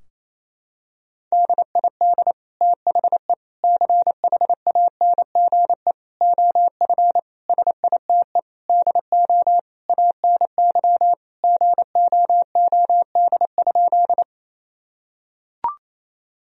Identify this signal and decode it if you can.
Morse code, heard as a keyed tone, about 28 wpm, 700 Hz. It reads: DID THE CHANGE OF SITE DO ANY GOOD?
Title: Sentences from Top 1000 Words - Review Part 4 - 28wpm